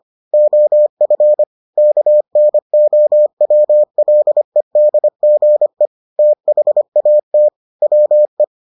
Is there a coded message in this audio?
Yes: Morse code from 0.3 to 8.4 s, heard as a keyed tone, about 25 wpm, 600 Hz. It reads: OF KNOWLEDGE THAT WE